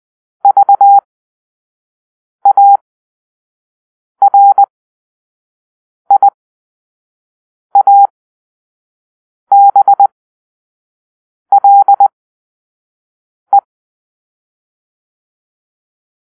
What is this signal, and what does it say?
Morse code, heard as a keyed tone, about 20 wpm, spaced as for 5 wpm, 800 Hz. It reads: VARIABLE
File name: arrl / Grp-5 Drill-4